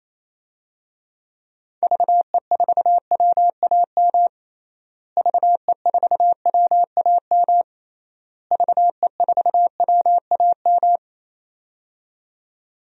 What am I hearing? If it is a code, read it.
Morse code, heard as a keyed tone, about 27 wpm, 700 Hz. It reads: VE4WAM VE4WAM VE4WAM
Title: DXCC-100 Call Signs - 6 Characters - Repeated 3x - 28wpm